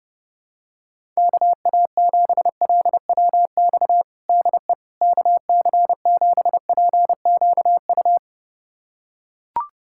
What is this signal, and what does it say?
Morse code, heard as a keyed tone, about 30 wpm, 700 Hz. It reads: KA7LWX DE KC7PQU
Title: USA Callsign DE Callsign - 6 Characters - 30wpm